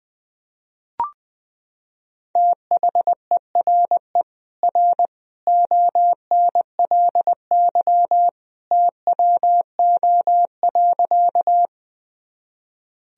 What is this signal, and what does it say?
Morse code, heard as a keyed tone, about 20 wpm, 700 Hz. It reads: THERE R ONLY TWO.
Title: Sentences from Top 200 Words and Abbreviations 20wpm